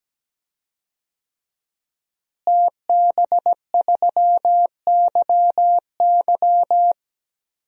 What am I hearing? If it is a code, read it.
Morse code, heard as a keyed tone, about 17 wpm, 700 Hz. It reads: TB3YY
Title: DXCC-100 Call Signs - 5 Characters - Repeated 2x - 17wpm